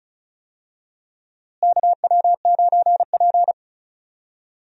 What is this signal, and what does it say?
Morse code, heard as a keyed tone, about 36 wpm, 700 Hz. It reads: KW9P
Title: USA Call Signs - 4 Characters - Repeated 3x - 35wpm